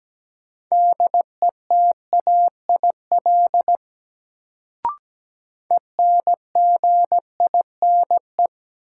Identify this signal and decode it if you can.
Morse code, heard as a keyed tone, about 17 wpm, 700 Hz. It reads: DETAIL ENGINE
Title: Sentences from Top 1500 Words - New Word Drill - 17wpm